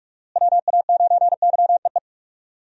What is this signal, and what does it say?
Morse code, heard as a keyed tone, about 46 wpm, 700 Hz. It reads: WA9YEE
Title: USA Call Signs - 6 Characters - Repeated 3x - 45wpm